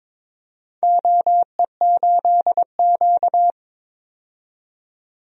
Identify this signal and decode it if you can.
Morse code, heard as a keyed tone, about 22 wpm, 700 Hz. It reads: OE8Q
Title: DXCC-100 Call Signs - 4 Characters - 22wpm